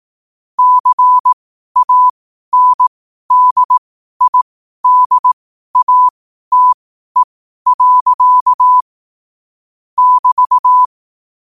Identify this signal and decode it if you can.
Morse code, heard as a keyed tone, about 18 wpm, spaced as for 12 wpm, 1000 Hz. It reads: CANDIDATE. <BT>
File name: qotdfiles / 12wpm